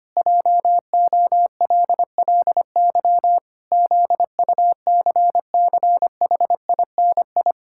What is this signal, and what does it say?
Morse code, heard as a keyed tone, about 25 wpm, 700 Hz. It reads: JOLLY ZUCCHINI
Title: Pangram Challenge v2 - 25wpm